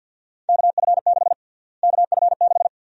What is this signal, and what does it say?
Morse code, heard as a keyed tone, about 48 wpm, 700 Hz. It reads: KF6 KF6
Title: USA Call Sign Prefixes - Repeated 2x - 50wpm